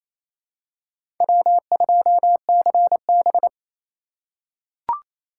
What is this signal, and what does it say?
Morse code, heard as a keyed tone, about 27 wpm, 700 Hz. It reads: W2CB
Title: USA Call Signs - 3x Character Spacing - 28wpm